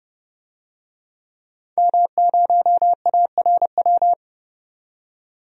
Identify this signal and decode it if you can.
Morse code, heard as a keyed tone, about 30 wpm, 700 Hz. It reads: M0ARW
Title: DXCC-100 Call Signs - 5 Characters - 30wpm